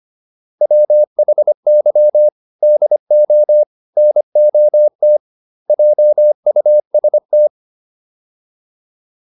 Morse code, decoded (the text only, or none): WHY DO NOT JUST